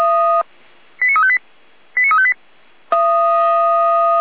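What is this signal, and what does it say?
Five-tone selective call (ZVEI), heard as quick runs of five tones, tones of about 70 ms. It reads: 89358 89358
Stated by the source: ZVEI five-tone alert received by SDR.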